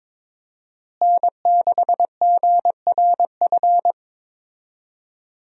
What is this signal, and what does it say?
Morse code, heard as a keyed tone, about 23 wpm, 700 Hz. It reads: N6GRF
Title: USA Call Signs - 5 Characters - Repeated 3x - 22wpm